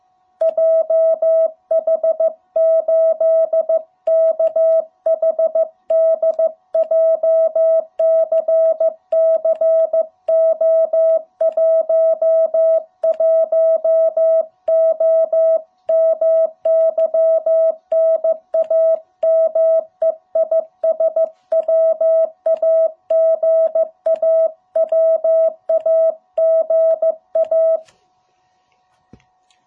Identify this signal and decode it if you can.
Morse code, heard as a keyed tone, about 15 wpm, 660 Hz. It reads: JH8KHD JCCO11O MY NAME IS WAGA WAGA
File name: jh8cbh / 2021cw